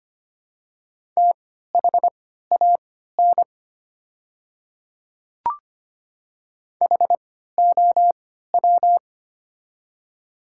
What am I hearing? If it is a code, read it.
Morse code, heard as a keyed tone, about 25 wpm, spaced as for 14 wpm, 700 Hz. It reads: THAN HOW
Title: 3x Character Spacing - Top 200 Words - 25wpm